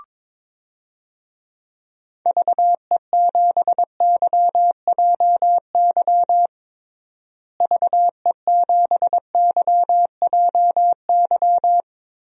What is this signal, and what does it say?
Morse code, heard as a keyed tone, about 22 wpm, 700 Hz. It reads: VE7YJY VE7YJY
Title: DXCC-100 Call Signs - 6 Characters - Repeated 2x - 22wpm